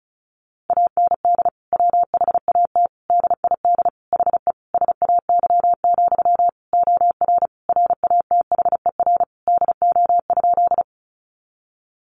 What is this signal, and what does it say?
Morse code, heard as a keyed tone, about 34 wpm, 700 Hz. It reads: AND WHAT DID HE SAY, OR RATHER DO?
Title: Sentences from Top 700 Words 35wpm